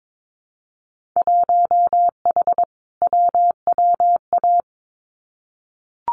Morse code, heard as a keyed tone, about 22 wpm, 700 Hz. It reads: 1H WWA
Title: Winter Field Day Exchanges 22wpm